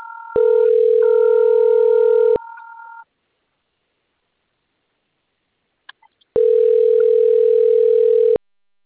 Ringback tone (North America), 440 + 480 Hz.
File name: rg-1600-unknown- - - .4646